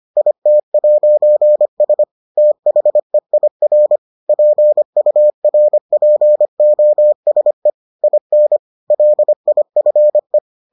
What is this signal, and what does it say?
Morse code, heard as a keyed tone, about 25 wpm, 600 Hz. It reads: IT'S THEIR PURPOSE IN LIFE